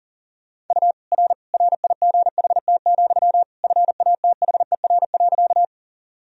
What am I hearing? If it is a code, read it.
Morse code, heard as a keyed tone, about 37 wpm, 700 Hz. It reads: U R RIGHT, FATHER.
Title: Sentences from Top 300 Words and Abbreviations 40wpm